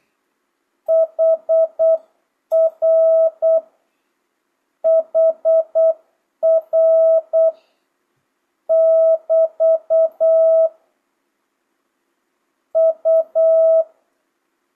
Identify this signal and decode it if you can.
Morse code, heard as a keyed tone, about 8 wpm, 650 Hz. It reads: HR HR <BT> U
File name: jh8cbh / 2025cw